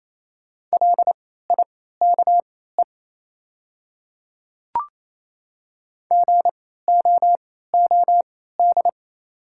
Morse code, heard as a keyed tone, about 27 wpm, spaced as for 16 wpm, 700 Hz. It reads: LIKE GOOD
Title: 3x Character Spacing - Top 500 Words - 28wpm